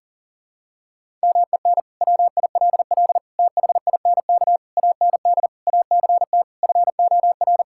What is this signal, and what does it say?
Morse code, heard as a keyed tone, about 40 wpm, 700 Hz. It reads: MEN WILL THINK AND ACT FOR
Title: Sentences from Top 1000 Words - Review Part 5 - 40wpm